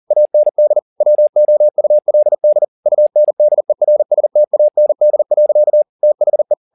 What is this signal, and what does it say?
Morse code, heard as a keyed tone, about 40 wpm, 600 Hz. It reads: AND WOULD UNDERSTAND. THE